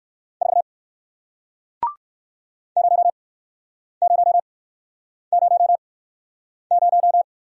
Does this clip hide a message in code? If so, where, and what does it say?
Yes: Morse code from 0.4 to 7.2 s, heard as a keyed tone, about 59 wpm, spaced as for 9 wpm, 700 Hz. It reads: 4 0000